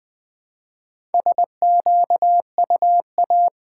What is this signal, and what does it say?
Morse code, heard as a keyed tone, about 20 wpm, 700 Hz. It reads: SQUA
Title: Build A Word - 2 Letters At A Time - 6-Letter Words - 20wpm